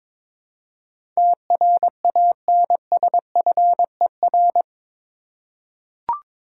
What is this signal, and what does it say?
Morse code, heard as a keyed tone, about 22 wpm, 700 Hz. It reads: TRANSFER
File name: practice / 22wpm / Eight Letter Words - 22wpm